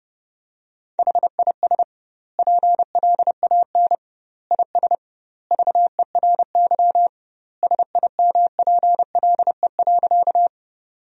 Morse code, heard as a keyed tone, about 30 wpm, 700 Hz. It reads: HIS PLAN IS VERY SIMPLE.
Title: 2x Word-Spacing - Sentences from Top 500 Words - 30wpm